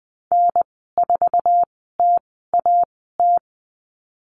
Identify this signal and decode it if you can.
Morse code, heard as a keyed tone, about 20 wpm, spaced as for 15 wpm, 700 Hz. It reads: N4TAT
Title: 048 Lesson 9g - Number 4 - Call Signs 20wpm